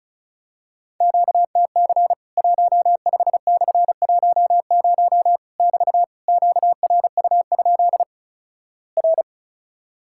Morse code, heard as a keyed tone, about 36 wpm, 700 Hz. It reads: QTC 15/10 <BT> QRU? R
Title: Worked All Europe DX Contest - QTC Exchange - 35wpm